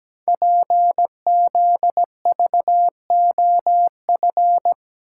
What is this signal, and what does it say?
Morse code, heard as a keyed tone, about 17 wpm, 700 Hz. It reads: PZVOF
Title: Code Groups - 5 Letters - 17wpm